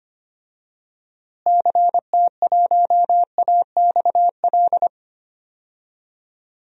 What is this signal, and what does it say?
Morse code, heard as a keyed tone, about 25 wpm, 700 Hz. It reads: CT1AXL
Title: DXCC-100 Call Signs - 6 Characters - Repeated 3x - 25wpm